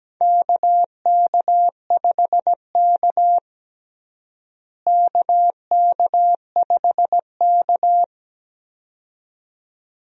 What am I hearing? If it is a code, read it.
Morse code, heard as a keyed tone, about 17 wpm, 700 Hz. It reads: KK5K KK5K
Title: USA Call Signs - 4 Characters - Repeated 2x - 17wpm